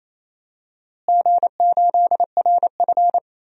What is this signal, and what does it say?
Morse code, heard as a keyed tone, about 28 wpm, 700 Hz. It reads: G8RF
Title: DXCC-100 Call Signs - 4 Characters - Repeated 3x - 28wpm